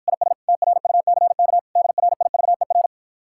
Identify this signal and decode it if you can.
Morse code, heard as a keyed tone, about 54 wpm, 700 Hz. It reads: IS TRUCK DRIVER